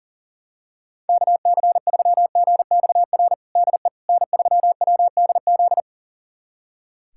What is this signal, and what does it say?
Morse code, heard as a keyed tone, about 40 wpm, 700 Hz. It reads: KC3GXR DE N3WDZ
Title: USA Callsign DE Callsign 40wpm